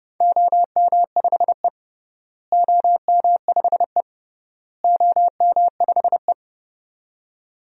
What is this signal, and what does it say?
Morse code, heard as a keyed tone, about 32 wpm, 700 Hz. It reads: OM5E OM5E OM5E
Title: DXCC-100 Call Signs - 4 Characters - Repeated 3x - 30wpm